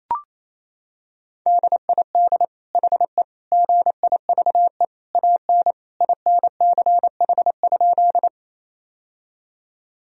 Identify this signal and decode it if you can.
Morse code, heard as a keyed tone, about 28 wpm, 700 Hz. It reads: DID HE GIVE AN INCH?
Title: Sentences from Top 1000 Words - Review Part 3 - 28wpm